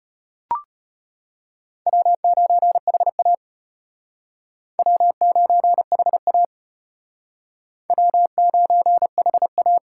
Morse code, heard as a keyed tone, about 32 wpm, 700 Hz. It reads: W9HA W9HA W9HA